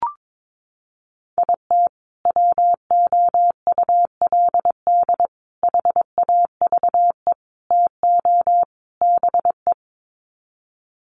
Morse code, heard as a keyed tone, about 22 wpm, 700 Hz. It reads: IT WOULD HAVE TO BE
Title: Sets of 5 Words Encore 22wpm